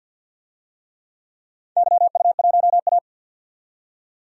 Morse code, heard as a keyed tone, about 53 wpm, 700 Hz. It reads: YU1A
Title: DXCC-100 Call Signs - 4 Characters - 50wpm